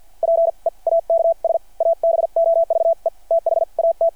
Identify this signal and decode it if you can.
Morse code, heard as a keyed tone, about 52 wpm, spaced as for 41 wpm, 660 Hz. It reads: PEAKS ABOVE THAT
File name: archived / CRSnet